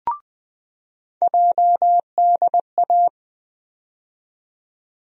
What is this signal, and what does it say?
Morse code, heard as a keyed tone, about 20 wpm, 700 Hz. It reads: JDA
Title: USA Call Sign Postfixes - 3 Letters 20wpm